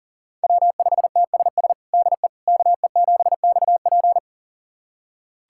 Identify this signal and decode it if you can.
Morse code, heard as a keyed tone, about 40 wpm, 700 Hz. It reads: W5TSS DE KE7XP
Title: USA Callsign DE Callsign 40wpm